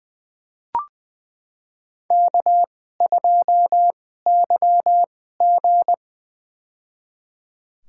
Morse code, heard as a keyed tone, about 20 wpm, spaced as for 14 wpm, 700 Hz. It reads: K2YG
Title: USA Call Signs - 4 Characters - 2x Character Spacing - 20wpm